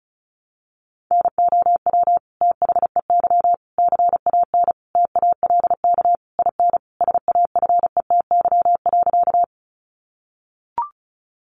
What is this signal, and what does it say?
Morse code, heard as a keyed tone, about 34 wpm, 700 Hz. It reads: NOW THEY CAN TALK IN SAFETY.
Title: Sentences from Top 700 Words 35wpm